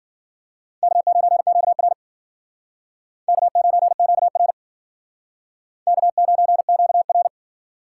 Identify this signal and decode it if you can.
Morse code, heard as a keyed tone, about 51 wpm, 700 Hz. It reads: K9QR K9QR K9QR